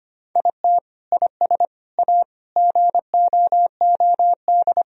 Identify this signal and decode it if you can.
Morse code, heard as a keyed tone, about 25 wpm, 700 Hz. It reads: IT IS A GOOD
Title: Sentences from Top 1000 Words - Review Part 3 - 25wpm